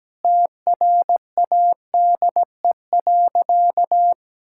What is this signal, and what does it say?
Morse code, heard as a keyed tone, about 17 wpm, 700 Hz. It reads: TRADE.
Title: Sentences from Top 1600 Words 17wpm